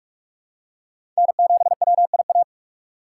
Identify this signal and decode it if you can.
Morse code, heard as a keyed tone, about 46 wpm, 700 Hz. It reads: N7WIA
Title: USA Call Signs - 3x Character Spacing - 45wpm